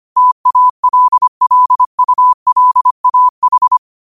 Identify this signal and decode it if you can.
Morse code, heard as a keyed tone, about 25 wpm, 1000 Hz. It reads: TALLULAH